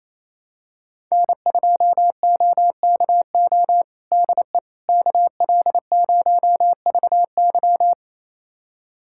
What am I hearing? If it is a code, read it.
Morse code, heard as a keyed tone, about 28 wpm, 700 Hz. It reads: N2OKO DE KL0VY
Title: USA Callsign DE Callsign 28wpm